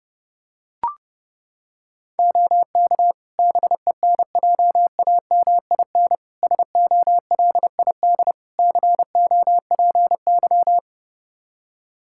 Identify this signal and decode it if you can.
Morse code, heard as a keyed tone, about 32 wpm, 700 Hz. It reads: OK BENJAMIN SOLID COPY
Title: K1USN SST QSO Practice 30wpm